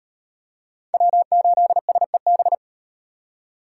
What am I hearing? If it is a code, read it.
Morse code, heard as a keyed tone, about 38 wpm, 700 Hz. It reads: W8SEB